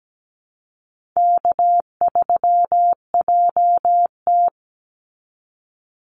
Morse code, heard as a keyed tone, about 17 wpm, 700 Hz. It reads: K3JT
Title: Easy Call Signs Repeated 2x - 17wpm